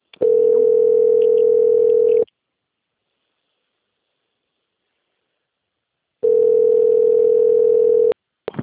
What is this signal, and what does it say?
Ringback tone (North America), 440 + 480 Hz.
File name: out- - - - - .13916